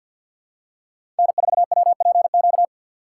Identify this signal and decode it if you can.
Morse code, heard as a keyed tone, about 52 wpm, 690 Hz. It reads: N4WPQ
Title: USA Call Signs - 5 Characters - 6x Character Spacing - 50wpm